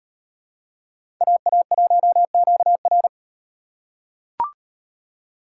Morse code, heard as a keyed tone, about 37 wpm, 700 Hz. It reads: AA1QR